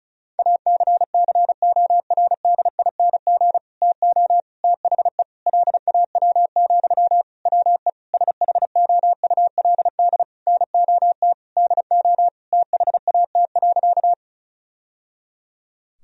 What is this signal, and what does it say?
Morse code, heard as a keyed tone, about 36 wpm, 700 Hz. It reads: ACCORDING TO THE LAW, WE SHOULD NOT DO THAT.
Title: Sentences from Top 1000 Words - Review Part 4 - 35wpm